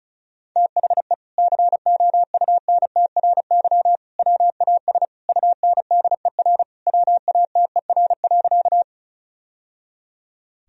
Morse code, heard as a keyed tone, about 36 wpm, 700 Hz. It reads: THE COUNTRY WAS UNDER WATER.
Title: Sentences from Top 1000 Words - Review Part 1 - 35wpm